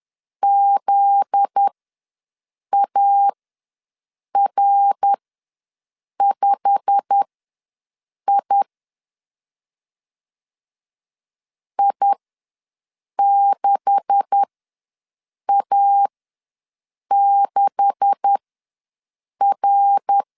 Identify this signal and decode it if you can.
Morse code, heard as a keyed tone, about 10 wpm, spaced as for 6 wpm, 800 Hz. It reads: ZAR5I I6A6R
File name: LEZIONI / 9 286 ws 10wpm 4032